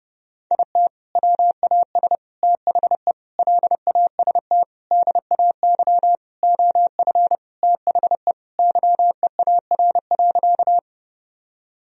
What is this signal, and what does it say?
Morse code, heard as a keyed tone, about 30 wpm, 700 Hz. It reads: IT WAS THE LAST DAY OF THE YEAR.
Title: Sentences from Top 1000 Words - Review Part 1 - 30wpm